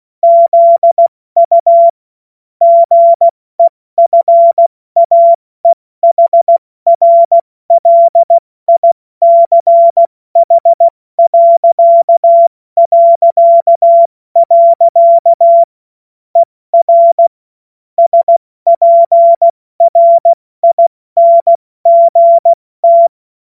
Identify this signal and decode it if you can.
Morse code, heard as a keyed tone, about 16 wpm, 680 Hz. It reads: ZU GEFAEHRLICH... ER SPRINGT
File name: Morsekurs / Witze / Witz 07. Dezember 2021, 70 F80